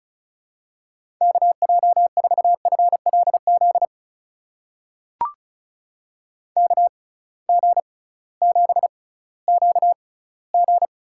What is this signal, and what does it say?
Morse code, heard as a keyed tone, about 34 wpm, spaced as for 19 wpm, 700 Hz. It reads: KJ4FLZ KG7QG
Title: USA Call Signs - 6 Characters - 6x Character Spacing - 35wpm